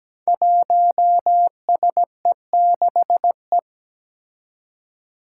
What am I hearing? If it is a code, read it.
Morse code, heard as a keyed tone, about 17 wpm, 700 Hz. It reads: 1SE6E
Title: Five Challenging Letter-Number Combinations 17wpm